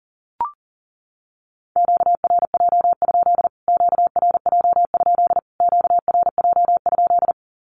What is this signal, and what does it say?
Morse code, heard as a keyed tone, about 40 wpm, 700 Hz. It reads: QRJ? QRJ? QRJ?
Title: Q-Codes Repeated 3x 40wpm